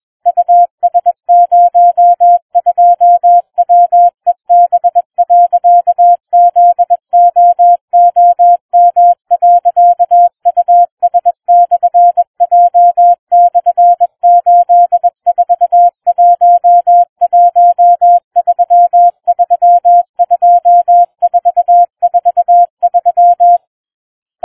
Morse code, heard as a keyed tone, about 21 wpm, 700 Hz. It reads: US02WEB.ZOOM.US/J/8411332443